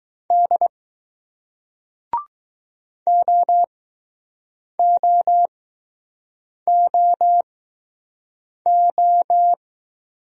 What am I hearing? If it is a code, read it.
Morse code, heard as a keyed tone, about 21 wpm, 700 Hz. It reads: D O O O O